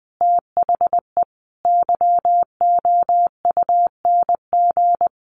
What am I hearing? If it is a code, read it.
Morse code, heard as a keyed tone, about 19 wpm, 700 Hz. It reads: THE YOUNG
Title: Sentences from Top 900 Words 20wpm